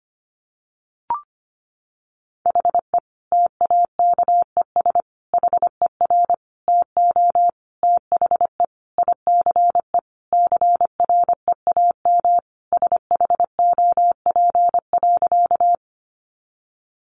Morse code, heard as a keyed tone, about 25 wpm, 700 Hz. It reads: HE TAKES HER TO THE ICE CREAM SHOP.
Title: Additional Easy Phrases 25wpm